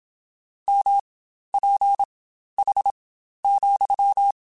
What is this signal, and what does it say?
Morse code, heard as a keyed tone, about 27 wpm, spaced as for 12 wpm, 800 Hz. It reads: MPH,
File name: Spaced Characters / Spaced Characters-mp3s / Weather 5 1 sp @25